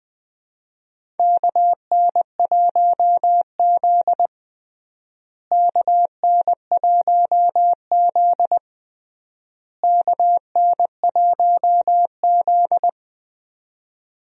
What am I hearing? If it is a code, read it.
Morse code, heard as a keyed tone, about 20 wpm, 700 Hz. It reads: KN1Z KN1Z KN1Z
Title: USA Call Signs - 4 Characters - Repeated 3x - 20wpm